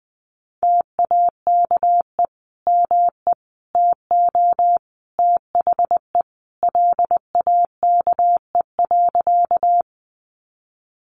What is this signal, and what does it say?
Morse code, heard as a keyed tone, about 20 wpm, 700 Hz. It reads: TAKE ME TO THE LAKE.
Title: Sentences from Top 1100 Words 20wpm